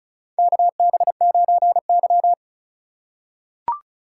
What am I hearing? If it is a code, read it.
Morse code, heard as a keyed tone, about 34 wpm, 700 Hz. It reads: KB9Y
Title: USA Call Signs - 2x Character Spacing - 35wpm